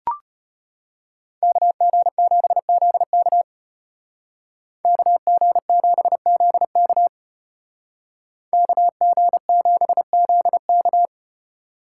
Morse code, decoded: KG7ZK KG7ZK KG7ZK

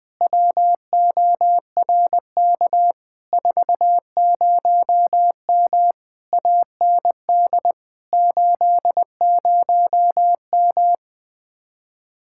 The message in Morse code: WORK 40M AND 80M

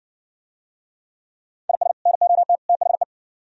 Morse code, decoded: IS NOT THE

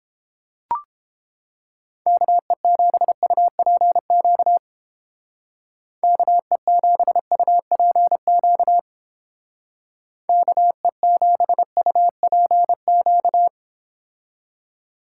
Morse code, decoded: KE7UPQ KE7UPQ KE7UPQ

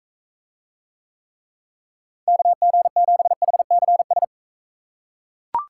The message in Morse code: KG7HCS

